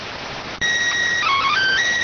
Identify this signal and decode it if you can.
SSTV (slow-scan television), PD 290.